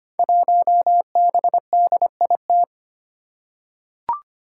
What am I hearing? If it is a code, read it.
Morse code, heard as a keyed tone, about 25 wpm, 700 Hz. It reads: 1BDIT
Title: Five Challenging Letter-Number Combinations 25wpm